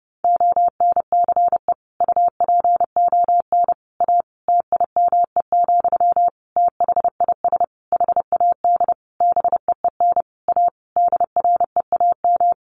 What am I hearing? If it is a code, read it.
Morse code, heard as a keyed tone, about 30 wpm, 700 Hz. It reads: ONCE UPON A TIME, THIS HAD BEEN A DREAM
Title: Sentences from Top 1500 Words 30wpm